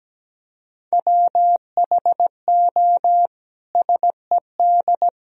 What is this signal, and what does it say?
Morse code, heard as a keyed tone, about 17 wpm, 700 Hz. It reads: WHO SED
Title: Sentences from Top 300 Words and Abbreviations 17wpm